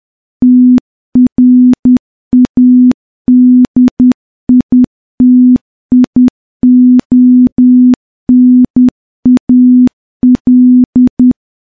Morse code, heard as a keyed tone, about 10 wpm, 250 Hz. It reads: TRADITIONAL
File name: morse-training-10WPM